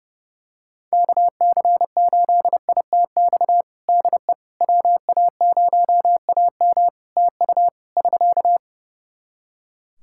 Morse code, heard as a keyed tone, about 30 wpm, 700 Hz. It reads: KC8ITX DE WA0AM TU <SK>
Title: Practice QSO - Exchange 4 - Ending - 30wpm